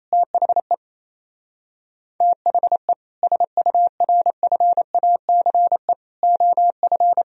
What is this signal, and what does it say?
Morse code, heard as a keyed tone, about 34 wpm, 700 Hz. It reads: THE THE SURFACE OF
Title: Speed-Racing - Sets of 4 Words 28wpm